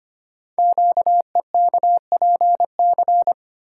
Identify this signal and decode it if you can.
Morse code, heard as a keyed tone, about 25 wpm, 700 Hz. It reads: QEKPC
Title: Code Groups - 5 Letters - 25wpm